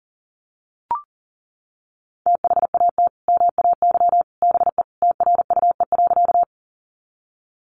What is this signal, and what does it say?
Morse code, heard as a keyed tone, about 37 wpm, 700 Hz. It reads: THAT MAY BE TRUE.
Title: Sentences from Top 400 Words 40wpm